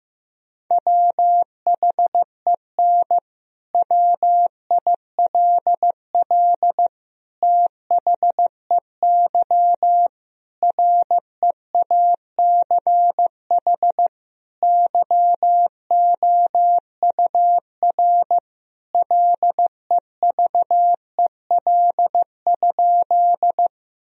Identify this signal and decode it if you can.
Morse code, heard as a keyed tone, about 16 wpm, 700 Hz. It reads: WHEN WILL THEY REACH YOUR LEVEL?
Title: Sentences from Top 1000 Words - Review Part 4 - 15wpm